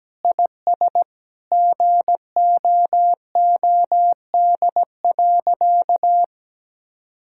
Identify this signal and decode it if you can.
Morse code, heard as a keyed tone, about 17 wpm, 700 Hz. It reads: IS GOOD.
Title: Sentences from Top 1000 Words - Review Part 6 - 17wpm